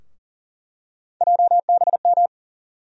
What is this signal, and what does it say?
Morse code, heard as a keyed tone, about 40 wpm, 700 Hz. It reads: JBM
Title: USA Call Sign Postfixes - 3 Letters 40wpm